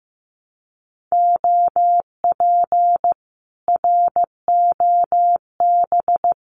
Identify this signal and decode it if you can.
Morse code, heard as a keyed tone, about 15 wpm, 700 Hz. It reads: OP ROB
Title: Common QSO Phrases Encore 15wpm